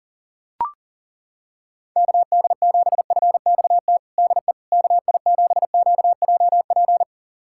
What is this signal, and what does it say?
Morse code, heard as a keyed tone, about 40 wpm, 700 Hz. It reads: KD7FXT DE KI7QJP
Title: USA Callsign DE Callsign - 6 Characters - 40wpm